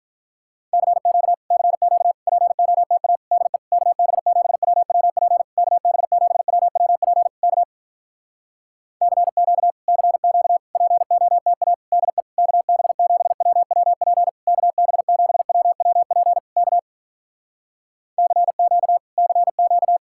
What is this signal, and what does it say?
Morse code, heard as a keyed tone, about 46 wpm, 700 Hz. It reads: CQ CQ POTA DE KB7WWP KB7WWP K CQ CQ POTA DE KB7WWP KB7WWP K CQ CQ